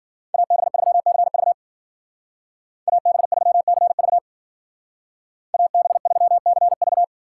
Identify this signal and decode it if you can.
Morse code, heard as a keyed tone, about 48 wpm, 700 Hz. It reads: AB3CV AB3CV AB3CV
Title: Speed Racing - USA 2x2 Call Signs 40wpm